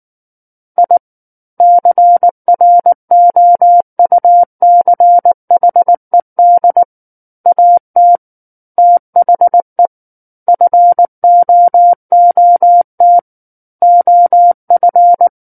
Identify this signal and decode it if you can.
Morse code, heard as a keyed tone, about 19 wpm, 700 Hz. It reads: I CROUCHED AT THE FOOT OF
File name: Book One / 10-Chapter 10 - 19 WPM